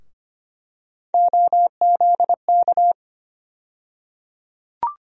Morse code, heard as a keyed tone, about 25 wpm, 700 Hz. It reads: OZK
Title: USA Call Sign Postfixes - 3 Letters 25wpm